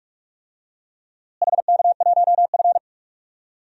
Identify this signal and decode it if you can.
Morse code, heard as a keyed tone, about 46 wpm, 700 Hz. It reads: HK1F
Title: DXCC-100 Call Signs - 4 Characters - Repeated 3x - 45wpm